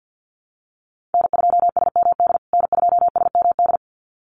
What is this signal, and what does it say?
Morse code, heard as a keyed tone, about 50 wpm, 700 Hz. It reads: N2SMD N2SMD
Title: USA Call Signs Repeated 2x 50wpm